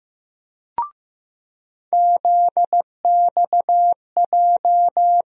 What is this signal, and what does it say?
Morse code, heard as a keyed tone, about 15 wpm, 700 Hz. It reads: ZXJ
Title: USA Call Sign Postfixes - 3 Letters 15wpm